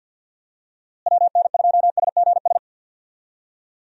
Morse code, heard as a keyed tone, about 52 wpm, 700 Hz. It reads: WN2SGS
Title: USA Call Signs - 6 Characters - Repeated 3x - 50wpm